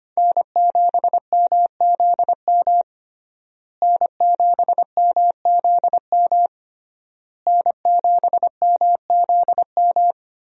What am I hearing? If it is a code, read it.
Morse code, heard as a keyed tone, about 25 wpm, 700 Hz. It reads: N7MZM N7MZM N7MZM
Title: USA Call Signs - 5 Characters - Repeated 3x - 25wpm